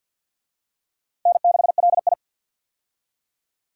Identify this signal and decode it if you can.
Morse code, heard as a keyed tone, about 47 wpm, 700 Hz. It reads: N6LI
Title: USA Call Signs - 4 Characters - 50wpm